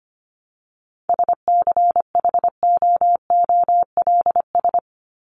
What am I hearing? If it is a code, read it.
Morse code, heard as a keyed tone, about 25 wpm, 700 Hz. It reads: SCHOOLS
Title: ES:S Suffix - Top 100 Words with Base Words 25wpm